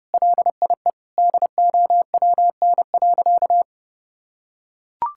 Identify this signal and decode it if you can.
Morse code, heard as a keyed tone, about 30 wpm, 700 Hz. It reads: LIE DOWN.
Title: Sentences from Top 1600 Words 30wpm